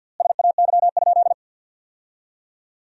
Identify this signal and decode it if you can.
Morse code, heard as a keyed tone, about 49 wpm, 700 Hz. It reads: SAY?
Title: 3x Word-Spacing - Sentences from Top 500 Words - 50wpm